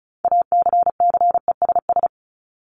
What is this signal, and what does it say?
Morse code, heard as a keyed tone, about 34 wpm, 700 Hz. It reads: ACCESS
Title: Top 501-750 Words 35wpm